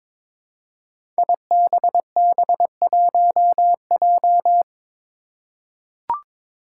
Morse code, heard as a keyed tone, about 22 wpm, 700 Hz. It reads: IBB1J
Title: Five Challenging Letter-Number Combinations 22wpm